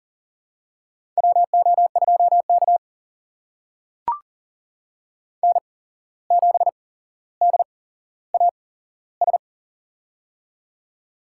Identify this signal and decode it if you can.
Morse code, heard as a keyed tone, about 41 wpm, spaced as for 16 wpm, 700 Hz. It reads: WO2K N7DAS
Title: USA Call Signs - 8x Character Spacing - 40wpm